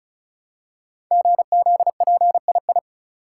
Morse code, heard as a keyed tone, about 36 wpm, 700 Hz. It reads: GZPII
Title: Code Groups - 5 Letter:Numbers - 35wpm